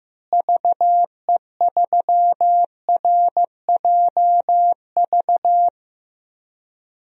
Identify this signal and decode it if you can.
Morse code, heard as a keyed tone, about 15 wpm, 700 Hz. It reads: VE3RJV
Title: DXCC-100 Call Signs - 6 Characters - Repeated 2x - 15wpm